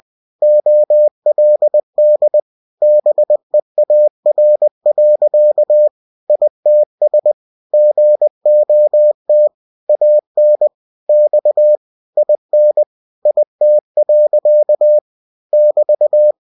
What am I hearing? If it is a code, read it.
Morse code, heard as a keyed tone, about 20 wpm, 600 Hz. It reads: OLD BEAR. ITS GOT AN X IN IT. <BT>